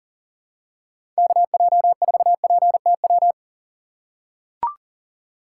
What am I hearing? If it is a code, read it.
Morse code, heard as a keyed tone, about 39 wpm, 700 Hz. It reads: KJ4PTW
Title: USA Call Signs - 2x Character Spacing - 40wpm